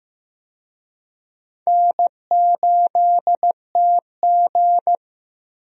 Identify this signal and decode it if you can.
Morse code, heard as a keyed tone, about 15 wpm, 700 Hz. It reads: N8TG
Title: USA Call Signs - 4 Characters - Repeated 3x - 15wpm